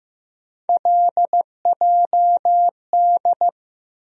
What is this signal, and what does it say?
Morse code, heard as a keyed tone, about 15 wpm, 700 Hz. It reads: LJD
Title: USA Call Sign Postfixes - 3 Letters 15wpm